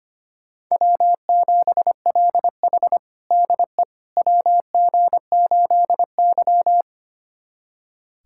Morse code, heard as a keyed tone, about 25 wpm, 700 Hz. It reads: W7LH DE WG8Y
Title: USA Callsign DE Callsign - 4 Characters - 25wpm